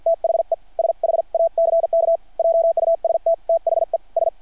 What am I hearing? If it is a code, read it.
Morse code, heard as a keyed tone, about 53 wpm, spaced as for 45 wpm, 660 Hz. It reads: THE SHACK JUST THE S
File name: archived / CRSnet